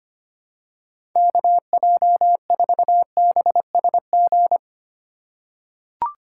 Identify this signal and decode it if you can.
Morse code, heard as a keyed tone, about 25 wpm, 700 Hz. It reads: KJ4BSG
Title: USA Call Signs - 6 Characters - 4x Character Spacing - 25wpm